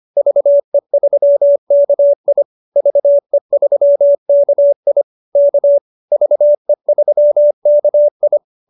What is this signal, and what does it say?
Morse code, heard as a keyed tone, about 25 wpm, 590 Hz. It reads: VE3KI VE3KI K VE3KI